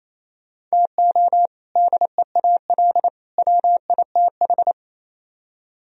Morse code, heard as a keyed tone, about 38 wpm, 700 Hz. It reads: TO DEAL WITH